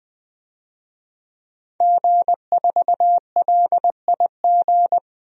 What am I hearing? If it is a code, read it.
Morse code, heard as a keyed tone, about 20 wpm, 700 Hz. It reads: G4LIG
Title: DXCC-100 Call Signs - 5 Characters - 20wpm